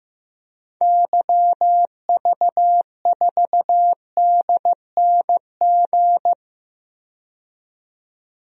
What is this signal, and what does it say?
Morse code, heard as a keyed tone, about 15 wpm, 700 Hz. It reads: YV4DNG
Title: DXCC-100 Call Signs - 6 Characters - 15wpm